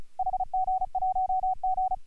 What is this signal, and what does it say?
Morse code, heard as a keyed tone, about 33 wpm, 750 Hz. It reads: HG1G
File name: AUDIO2018 / HG1G CW